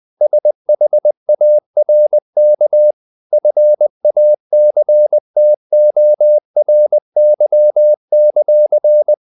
Morse code, heard as a keyed tone, about 20 wpm, 600 Hz. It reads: SHARK FACTORY;
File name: the old man and the sea / the old man and the sea part 1 20wpm